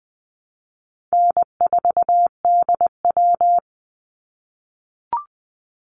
Morse code, heard as a keyed tone, about 20 wpm, 700 Hz. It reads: N4DW
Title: Call Signs - 20wpm